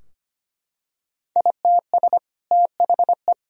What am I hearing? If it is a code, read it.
Morse code, heard as a keyed tone, about 25 wpm, 700 Hz. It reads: ITS THE
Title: Sentences from Top 1000 Words - Review Part 3 - 25wpm